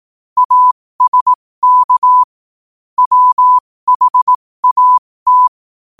Morse code, heard as a keyed tone, about 18 wpm, spaced as for 15 wpm, 1000 Hz. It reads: ASK WHAT